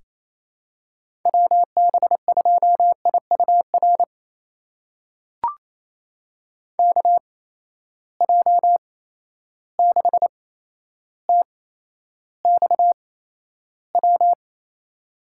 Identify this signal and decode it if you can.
Morse code, heard as a keyed tone, about 28 wpm, spaced as for 12 wpm, 700 Hz. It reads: WB2IUR KJ6TXW
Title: USA Call Signs - 8x Character Spacing - 28wpm